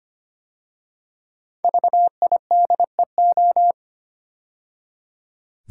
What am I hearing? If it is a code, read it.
Morse code, heard as a keyed tone, about 25 wpm, 700 Hz. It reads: VIDEO
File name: 5 Letter Words - First 3 Letters Given - 25wpm